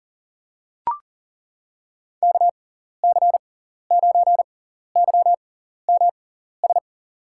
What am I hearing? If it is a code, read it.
Morse code, heard as a keyed tone, about 40 wpm, spaced as for 14 wpm, 700 Hz. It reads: KC9YMS
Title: USA Call Signs - 6x Character Spacing - 40wpm